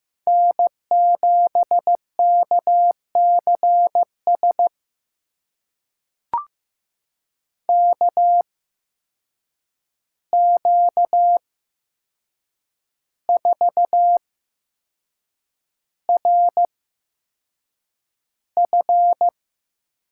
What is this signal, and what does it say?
Morse code, heard as a keyed tone, about 15 wpm, spaced as for 6 wpm, 700 Hz. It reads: N7KCS KQ4RF
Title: USA Call Signs - 5 Characters - 8x Character Spacing - 15wpm